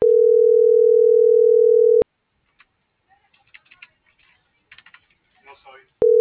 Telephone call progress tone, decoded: ringback tone

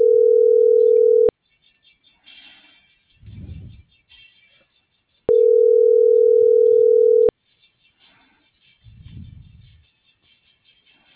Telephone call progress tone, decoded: ringback tone

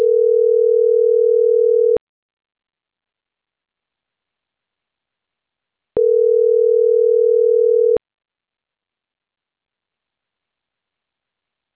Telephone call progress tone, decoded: ringback tone

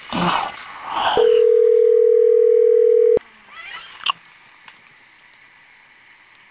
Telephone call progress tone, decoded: ringback tone

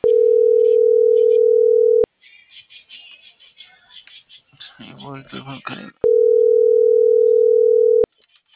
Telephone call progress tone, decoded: ringback tone

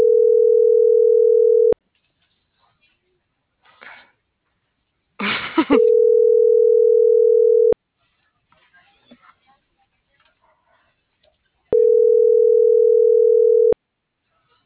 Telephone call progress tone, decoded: ringback tone

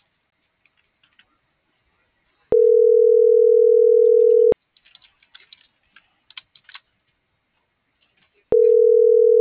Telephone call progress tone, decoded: ringback tone

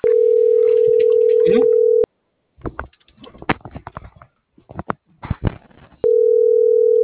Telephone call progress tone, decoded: ringback tone

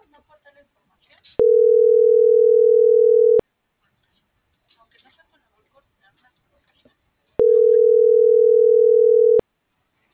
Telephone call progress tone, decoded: ringback tone

